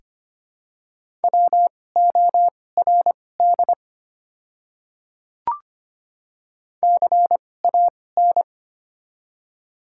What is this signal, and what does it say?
Morse code, heard as a keyed tone, about 25 wpm, spaced as for 18 wpm, 700 Hz. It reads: WORD CAN
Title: 2x Character Spacing - Top 100 Words - 25wpm